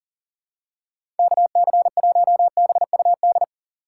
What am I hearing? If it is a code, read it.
Morse code, heard as a keyed tone, about 40 wpm, 700 Hz. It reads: KC1BUD
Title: USA Call Signs - 6 Characters - 3x Character Spacing - 40wpm